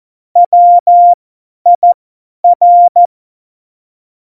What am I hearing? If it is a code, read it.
Morse code, heard as a keyed tone, about 14 wpm, spaced as for 10 wpm, 700 Hz. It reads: WIR
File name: Morsekurs / Witze / Witz 05. Juni 2018, 50 F70